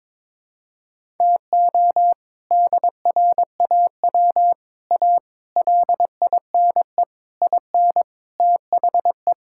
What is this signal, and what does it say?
Morse code, heard as a keyed tone, about 22 wpm, 700 Hz. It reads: TO DRAW A LINE IN THE
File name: practice / 22wpm / Sets of 7 Words - 22wpm